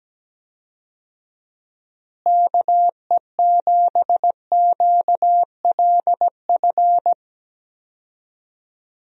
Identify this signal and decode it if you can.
Morse code, heard as a keyed tone, about 17 wpm, 700 Hz. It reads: KE7QLF
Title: USA Call Signs - 6 Characters - Repeated 2x - 17wpm